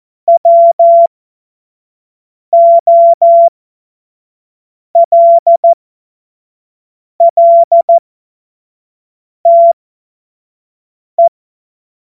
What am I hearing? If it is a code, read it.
Morse code, heard as a keyed tone, about 14 wpm, spaced as for 5 wpm, 680 Hz. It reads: WOLLTE
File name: Morsekurs / Witze / Witz 14. Februar 2023 25 F70